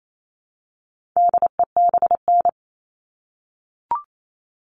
Morse code, heard as a keyed tone, about 28 wpm, 700 Hz. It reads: DEBN
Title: All Cut Numbers - 4-Digit 28wpm